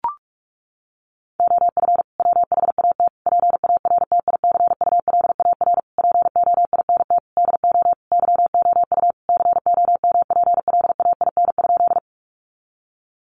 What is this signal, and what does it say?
Morse code, heard as a keyed tone, about 44 wpm, 700 Hz. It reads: OF WHAT PARTICULAR POINT DO YOU COMPLAIN?
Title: Sentences from Top 1300 Words 45wpm